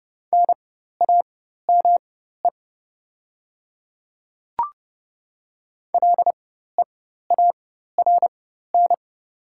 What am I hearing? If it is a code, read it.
Morse code, heard as a keyed tone, about 30 wpm, spaced as for 14 wpm, 700 Hz. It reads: NAME LEARN
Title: 4x Character Spacing - Top 500 Words - 30wpm